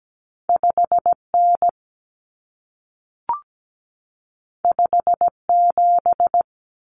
Morse code, heard as a keyed tone, about 17 wpm, 700 Hz. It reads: 5N 57
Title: Common Cut Numbers - 2-Digit 17wpm